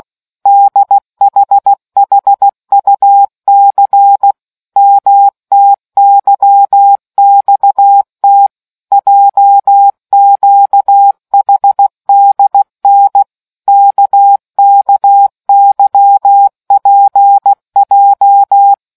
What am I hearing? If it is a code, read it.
Morse code, heard as a keyed tone, about 16 wpm, 800 Hz. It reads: DHHUC MTYXT JQHDN KKYPJ